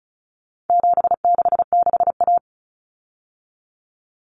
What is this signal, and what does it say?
Morse code, heard as a keyed tone, about 34 wpm, 700 Hz. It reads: 766A
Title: Common Cut Numbers - 4-Digit 35wpm